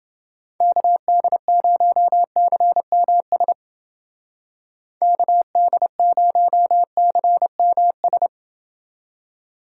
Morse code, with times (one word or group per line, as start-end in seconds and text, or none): 0.6-3.5 KD0CMS
5.0-8.3 KD0CMS